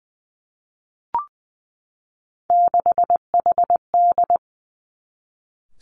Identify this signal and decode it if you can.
Morse code, heard as a keyed tone, about 20 wpm, 700 Hz. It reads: 6HD
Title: Three Challenging Letter-Number Combinations 20wpm